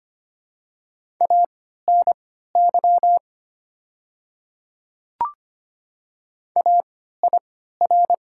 Morse code, heard as a keyed tone, about 25 wpm, spaced as for 14 wpm, 700 Hz. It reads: ANY AIR
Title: 3x Character Spacing - Top 200 Words - 25wpm